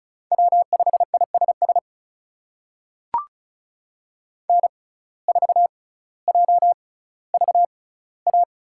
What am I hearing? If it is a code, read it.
Morse code, heard as a keyed tone, about 36 wpm, spaced as for 18 wpm, 700 Hz. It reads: W5ISS N4JVA
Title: USA Call Signs - 5 Characters - 6x Character Spacing - 35wpm